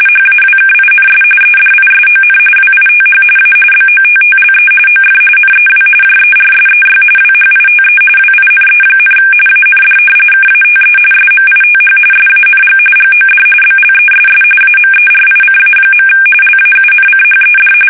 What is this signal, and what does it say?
RTTY, about 45.45 baud. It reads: HAWKINS EDISON WITH A BOTTLE OF NEW JERSEY WHITE WINE, AND KATE SMITH SANG GOD BLESS AMERICA. LAUNCHED
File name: fsk850 / battleshipnj